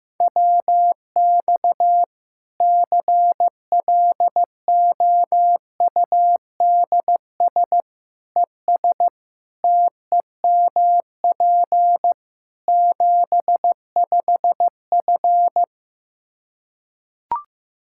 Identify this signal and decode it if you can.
Morse code, heard as a keyed tone, about 15 wpm, 700 Hz. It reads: WX CLOUDS ES TEMP 75F
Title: K1USN SST QSO Practice 15wpm